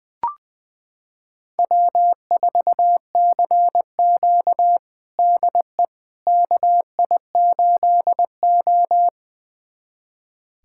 Morse code, heard as a keyed tone, about 20 wpm, 700 Hz. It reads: W4CQ DE KI8O
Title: USA Callsign DE Callsign - 4 Characters - 20wpm